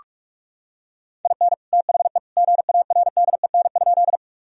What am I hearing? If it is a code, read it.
Morse code, heard as a keyed tone, about 46 wpm, 700 Hz. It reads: IN THE GARDEN?
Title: Sentences from Top 1000 Words - Review Part 4 - 45wpm